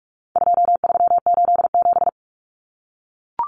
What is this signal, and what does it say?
Morse code, heard as a keyed tone, about 44 wpm, 700 Hz. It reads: 2387
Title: Common Cut Numbers - 4-Digit 45wpm